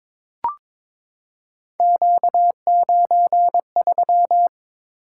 Morse code, heard as a keyed tone, about 22 wpm, 700 Hz. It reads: Q93
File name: practice / 22wpm / Three Letter-Number Combinations - 22wpm